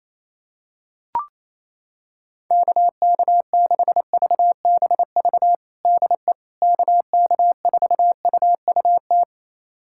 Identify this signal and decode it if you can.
Morse code, heard as a keyed tone, about 27 wpm, 700 Hz. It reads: KK6VBV DE KK4UUT
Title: USA Callsign DE Callsign - 6 Characters - 28wpm